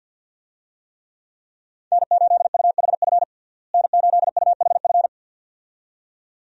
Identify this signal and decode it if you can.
Morse code, heard as a keyed tone, about 50 wpm, 700 Hz. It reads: N8UHF N8UHF
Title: USA Call Signs - 5 Characters - Repeated 2x - 50wpm